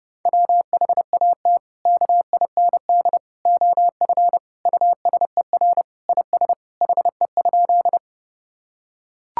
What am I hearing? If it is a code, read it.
Morse code, heard as a keyed tone, about 30 wpm, 700 Hz. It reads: WHAT KIND OF USER IS HE?
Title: Sentences from Top 1000 Words - Review Part 5 - 30wpm